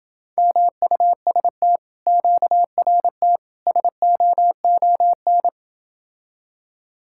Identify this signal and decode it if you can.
Morse code, heard as a keyed tone, about 28 wpm, 700 Hz. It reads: MUST QRT SOON